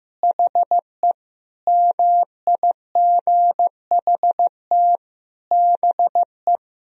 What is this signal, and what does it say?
Morse code, heard as a keyed tone, about 15 wpm, 700 Hz. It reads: HE MIGHT BE
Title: Sentences from Top 1000 Words - Review Part 4 - 15wpm